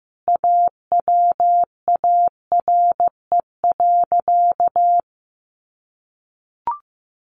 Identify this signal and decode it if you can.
Morse code, heard as a keyed tone, about 15 wpm, 700 Hz. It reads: AWARE.
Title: Sentences from Top 1300 Words 15wpm